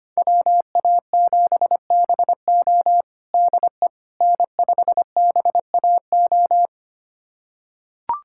Morse code, heard as a keyed tone, about 25 wpm, 700 Hz. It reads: WA7BO DE N5BAO
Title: USA Callsign DE Callsign - 5 Characters - 25wpm